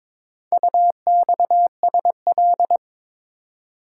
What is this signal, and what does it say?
Morse code, heard as a keyed tone, about 22 wpm, 700 Hz. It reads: UXSL
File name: practice / 22wpm / Four Letter-Number Combinations - 22wpm